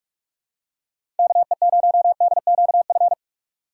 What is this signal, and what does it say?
Morse code, heard as a keyed tone, about 46 wpm, 700 Hz. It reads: KE0DQF
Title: USA Call Signs - 6 Characters - 2x Character Spacing - 45wpm